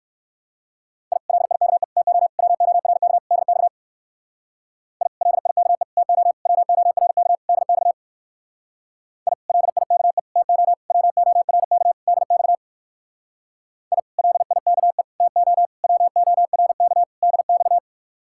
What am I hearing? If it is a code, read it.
Morse code, heard as a keyed tone, about 58 wpm, 700 Hz. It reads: I LIKE TO WORK DX I LIKE TO WORK DX I LIKE TO WORK DX I LIKE TO WORK DX